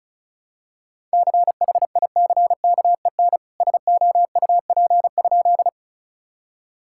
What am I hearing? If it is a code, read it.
Morse code, heard as a keyed tone, about 36 wpm, 700 Hz. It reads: CHICKEN SOUP?
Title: Sentences from Top 1000 Words - Review Part 6 - 35wpm